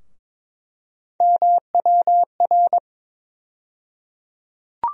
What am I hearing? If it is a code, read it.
Morse code, heard as a keyed tone, about 22 wpm, 700 Hz. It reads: MWR
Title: USA Call Sign Postfixes - 3 Letters 22wpm